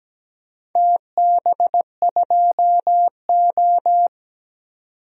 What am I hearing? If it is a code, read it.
Morse code, heard as a keyed tone, about 17 wpm, 700 Hz. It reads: TB2O
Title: DXCC-100 Call Signs - 4 Characters - 17wpm